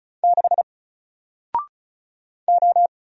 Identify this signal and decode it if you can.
Morse code, heard as a keyed tone, about 36 wpm, 700 Hz. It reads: 6 O
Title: Single Letter-Number Alt 35wpm